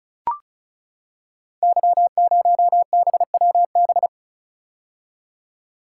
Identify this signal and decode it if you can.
Morse code, heard as a keyed tone, about 36 wpm, 700 Hz. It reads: Y0BWB
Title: Code Groups - 5 Letter:Numbers - 35wpm